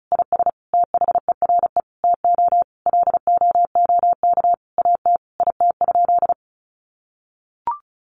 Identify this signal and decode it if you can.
Morse code, heard as a keyed tone, about 34 wpm, 700 Hz. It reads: IS THERE TO LOOK AT IT?
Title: Sentences from Top 100 Words 35wpm